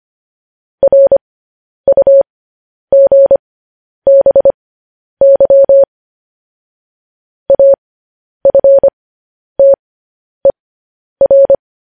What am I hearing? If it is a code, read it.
Morse code, heard as a keyed tone, about 25 wpm, spaced as for 10 wpm, 550 Hz. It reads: RUGBY AFTER
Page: news headlines in morse code at 10 wpm